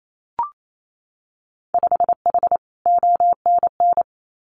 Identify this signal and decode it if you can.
Morse code, heard as a keyed tone, about 28 wpm, 700 Hz. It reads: 5H ONN
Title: Winter Field Day Exchanges Encore 28wpm